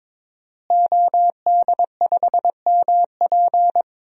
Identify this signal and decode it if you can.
Morse code, heard as a keyed tone, about 22 wpm, 700 Hz. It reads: OD5MP
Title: DXCC-100 Call Signs - 5 Characters - Repeated 2x - 22wpm